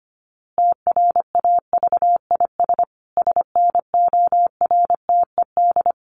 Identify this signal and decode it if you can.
Morse code, heard as a keyed tone, about 25 wpm, 700 Hz. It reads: TRAVIS SNORTED
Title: A Sound Of Thunder 25wpm